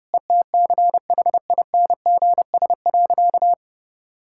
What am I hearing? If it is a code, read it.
Morse code, heard as a keyed tone, about 30 wpm, 700 Hz. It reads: ETCHINGS.
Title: Pangram Challenge v2 - 30wpm